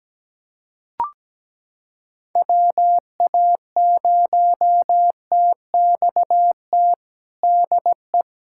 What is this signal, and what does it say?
Morse code, heard as a keyed tone, about 17 wpm, 700 Hz. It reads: WA0TXT DE
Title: USA Callsign DE Callsign 17wpm